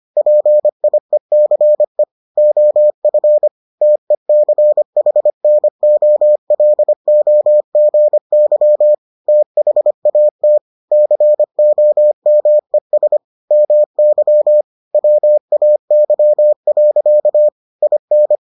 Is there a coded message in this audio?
Yes: Morse code from 0.2 to 18.4 s, heard as a keyed tone, about 25 wpm, 600 Hz. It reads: PIECE OF TECHNOLOGY THAT COMES MY WAY. IN